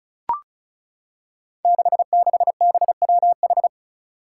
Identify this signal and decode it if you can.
Morse code, heard as a keyed tone, about 36 wpm, 700 Hz. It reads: 66BWH
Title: Five Challenging Letter-Number Combinations 35wpm